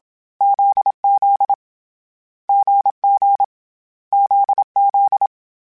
Morse code, heard as a keyed tone, about 27 wpm, 800 Hz. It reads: ZZ GG ZZ
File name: Difficult Combos / Difficult Combos-mp3s / GZ GZ7 @25